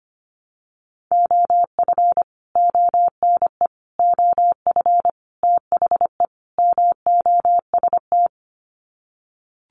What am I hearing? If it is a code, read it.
Morse code, heard as a keyed tone, about 25 wpm, 700 Hz. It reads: OF ONE OF THE MOST
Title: Sets of 5 Words Encore2 25wpm